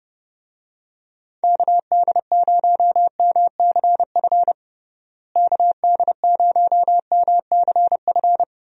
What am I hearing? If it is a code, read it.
Morse code, heard as a keyed tone, about 30 wpm, 700 Hz. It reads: KD0MCF KD0MCF
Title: USA Call Signs - 6 Characters - Repeated 2x - 30wpm